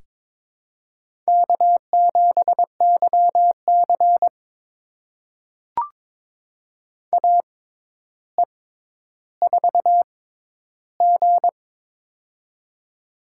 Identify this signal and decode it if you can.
Morse code, heard as a keyed tone, about 22 wpm, spaced as for 11 wpm, 700 Hz. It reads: K7YC AE4G
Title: USA Call Signs - 4 Characters - 6x Character Spacing - 22wpm